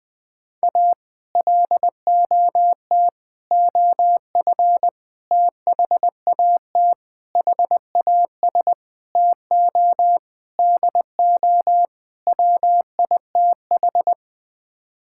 Morse code, heard as a keyed tone, about 20 wpm, 700 Hz. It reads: A LOT OF THAT HAS TO DO WITH
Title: Sets of 8 Words - 20wpm